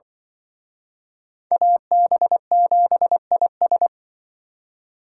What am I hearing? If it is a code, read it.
Morse code, heard as a keyed tone, about 25 wpm, 700 Hz. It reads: AB7IS
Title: Speed Racing - USA Call Signs 20wpm